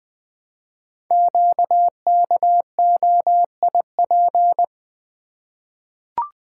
Morse code, heard as a keyed tone, about 20 wpm, 700 Hz. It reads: QKOIP
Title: Code Groups - 5 Letters - 20wpm